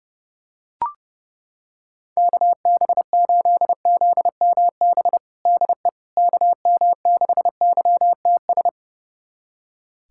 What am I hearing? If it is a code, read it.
Morse code, heard as a keyed tone, about 32 wpm, 700 Hz. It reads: KB8ZMB DE KM6YTS
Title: USA Callsign DE Callsign - 6 Characters - 30wpm